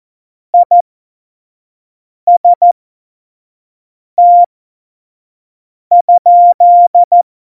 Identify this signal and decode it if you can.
Morse code, heard as a keyed tone, about 14 wpm, spaced as for 5 wpm, 700 Hz. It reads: IST?
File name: Morsekurs / Witze / Witz 23. Juli 2019, 25 F70